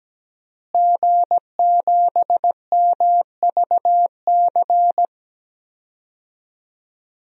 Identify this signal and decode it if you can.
Morse code, heard as a keyed tone, about 17 wpm, 700 Hz. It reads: G7MVC
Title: DXCC-100 Call Signs - 5 Characters - Repeated 3x - 17wpm